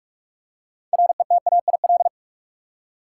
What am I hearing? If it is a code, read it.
Morse code, heard as a keyed tone, about 43 wpm, 700 Hz. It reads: RETAIL
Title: Practice QSO - Exchange 3 - More Info - 45wpm